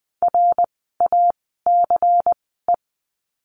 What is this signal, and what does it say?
Morse code, heard as a keyed tone, about 20 wpm, spaced as for 14 wpm, 700 Hz. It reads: RACE
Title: 094 Lesson 16f - Letter c - Words 20wpm